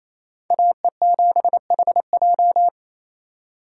Morse code, heard as a keyed tone, about 36 wpm, 700 Hz. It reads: AE7HJ